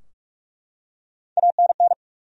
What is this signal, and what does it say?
Morse code, heard as a keyed tone, about 47 wpm, 700 Hz. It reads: ANN
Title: USA Common Names - Max 3 Letters - 45wpm